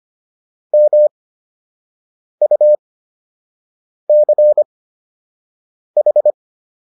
Morse code, heard as a keyed tone, about 25 wpm, spaced as for 6 wpm, 600 Hz. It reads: MUCH